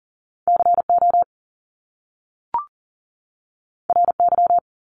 Morse code, heard as a keyed tone, about 39 wpm, 700 Hz. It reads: CO RY
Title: Top Two Letter Combinations in Words 40wpm